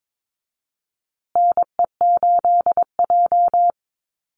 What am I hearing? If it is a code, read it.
Morse code, heard as a keyed tone, about 22 wpm, 690 Hz. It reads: NE8J
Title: Call Signs Repeated 3x - 22wpm